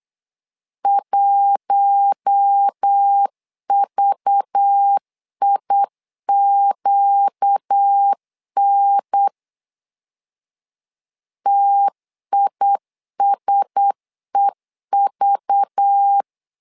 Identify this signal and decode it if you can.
Morse code, heard as a keyed tone, about 8 wpm, 800 Hz. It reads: 1VIQN TISEV
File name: LEZIONI / 5 LBVC / 5 LBV 8wpm 2837